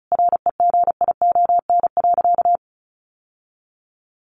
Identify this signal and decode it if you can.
Morse code, heard as a keyed tone, about 34 wpm, 700 Hz. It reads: REGION.
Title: Sentences from Top 1500 Words 35wpm